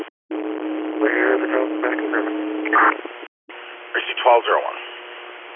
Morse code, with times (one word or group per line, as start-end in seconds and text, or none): none